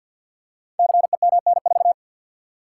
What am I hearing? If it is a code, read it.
Morse code, heard as a keyed tone, about 50 wpm, 700 Hz. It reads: CEMN4
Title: Code Groups - 5 Letter:Numbers - 50wpm